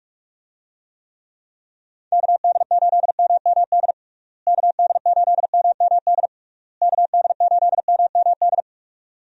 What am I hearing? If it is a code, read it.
Morse code, heard as a keyed tone, about 46 wpm, 700 Hz. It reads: KD8MMD KD8MMD KD8MMD
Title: USA Call Signs - 6 Characters - Repeated 3x - 45wpm